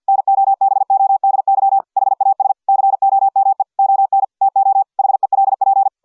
Morse code, heard as a keyed tone, about 50 wpm, 780 Hz. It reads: NOBODY HAS COME ON TO HELP